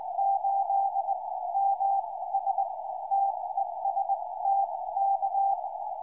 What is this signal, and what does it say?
Morse code, heard as a keyed tone, about 19 wpm, spaced as for 16 wpm, 790 Hz, with radio noise. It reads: 8 MS TEST K
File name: dj8ms 2m